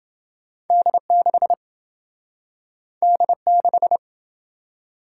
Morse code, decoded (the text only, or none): D6 D6